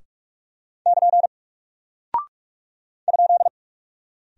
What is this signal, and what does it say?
Morse code, heard as a keyed tone, about 44 wpm, 700 Hz. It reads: <KN> ?
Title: Single Character (Letter:Number:Punctuation:Prosign) Alt 45wpm